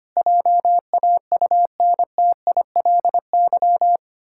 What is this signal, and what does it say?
Morse code, heard as a keyed tone, about 25 wpm, 700 Hz. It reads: JAUNTILY
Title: Pangram Challenge v2 - 25wpm